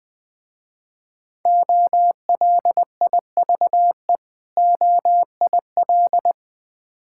Morse code, heard as a keyed tone, about 20 wpm, 700 Hz. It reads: OLIVE OIL
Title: Difficult Words with Hints - 20wpm